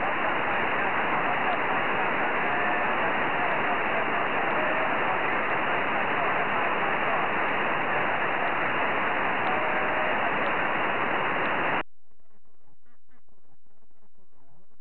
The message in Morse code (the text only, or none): none